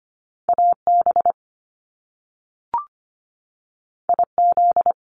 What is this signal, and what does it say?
Morse code, heard as a keyed tone, about 25 wpm, 700 Hz. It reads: AB IZ
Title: Top Two Letter Combinations in Words 25wpm